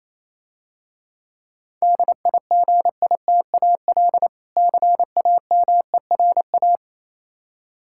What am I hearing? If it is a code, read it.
Morse code, heard as a keyed tone, about 27 wpm, 700 Hz. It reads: DIGITAL CAMERA
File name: Easy Words - Speed-Racing with Hints - 22wpm